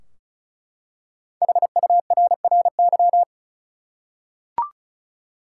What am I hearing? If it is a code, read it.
Morse code, heard as a keyed tone, about 36 wpm, 700 Hz. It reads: HURRY
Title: Five Letter Words - 35wpm